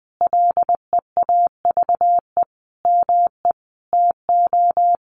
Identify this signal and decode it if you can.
Morse code, heard as a keyed tone, about 20 wpm, 700 Hz. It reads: LEAVE ME TO
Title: Sentences from Top 1300 Words 20wpm